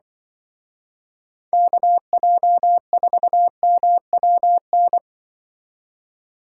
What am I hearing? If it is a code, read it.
Morse code, heard as a keyed tone, about 24 wpm, 700 Hz. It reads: KJ4MWN
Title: Speed Racing - USA Call Signs 20wpm